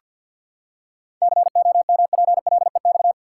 Morse code, heard as a keyed tone, about 47 wpm, 700 Hz. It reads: COMPLEX
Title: Seven Letter Words - 50wpm